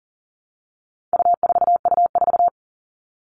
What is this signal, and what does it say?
Morse code, heard as a keyed tone, about 40 wpm, 700 Hz. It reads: U4U4
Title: All Cut Numbers - 4-Digit 40wpm